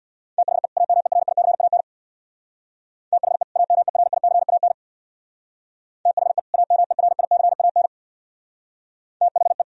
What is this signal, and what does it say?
Morse code, heard as a keyed tone, about 62 wpm, 700 Hz. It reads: THE AMERICAN THE AMERICAN THE AMERICAN THE
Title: Speed-Racing - Sets of 2 Words 50wpm